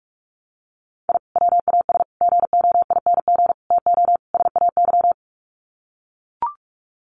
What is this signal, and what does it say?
Morse code, heard as a keyed tone, about 44 wpm, 700 Hz. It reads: I WAS GOING TO SAY
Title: Sets of 5 Words 45wpm